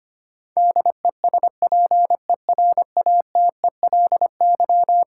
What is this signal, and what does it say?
Morse code, heard as a keyed tone, about 25 wpm, 700 Hz. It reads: DESPERATELY
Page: morse-code-ninja-practice-25wpm